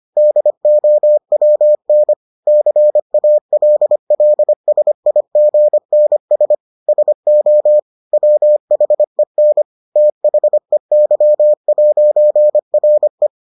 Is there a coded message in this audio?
Yes: Morse code from 0.2 to 13.3 s, heard as a keyed tone, about 25 wpm, 600 Hz. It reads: DOWN CALLSIGNS SO WHEN THEY'RE